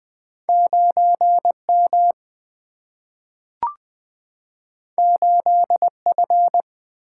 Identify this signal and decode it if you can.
Morse code, heard as a keyed tone, about 20 wpm, 700 Hz. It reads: 9 M 8 F
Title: Single-Letter Single-Number 20wpm